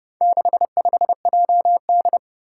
Morse code, heard as a keyed tone, about 29 wpm, 700 Hz. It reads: 65JD
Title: Four Challenging Letter-Number Combinations 30wpm